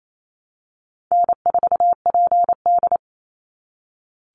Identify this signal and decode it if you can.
Morse code, heard as a keyed tone, about 28 wpm, 700 Hz. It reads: N4PD
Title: Easy Call Signs - 28wpm